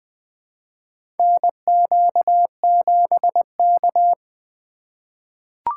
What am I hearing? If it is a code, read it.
Morse code, heard as a keyed tone, about 21 wpm, 700 Hz. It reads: NQ7K
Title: USA Call Signs - 4 Characters - 3x Character Spacing - 20wpm